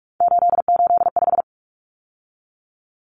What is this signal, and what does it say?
Morse code, heard as a keyed tone, about 44 wpm, 700 Hz. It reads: 885
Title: Numbers - 3-Digit 45wpm